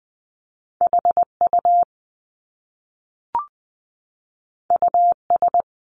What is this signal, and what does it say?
Morse code, heard as a keyed tone, about 20 wpm, 700 Hz. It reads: HU US
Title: Top Two Letter Combinations in Words 20wpm